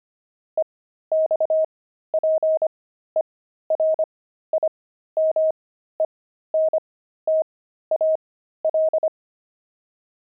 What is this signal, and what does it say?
Morse code, heard as a keyed tone, about 25 wpm, spaced as for 13 wpm, 630 Hz. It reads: EXPERIMENTAL
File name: pangram-fs-13-cs-25-zany